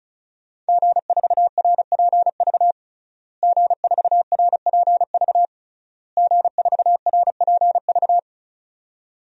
Morse code, heard as a keyed tone, about 36 wpm, 700 Hz. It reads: G4RPV G4RPV G4RPV
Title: DXCC-100 Call Signs - 5 Characters - Repeated 3x - 35wpm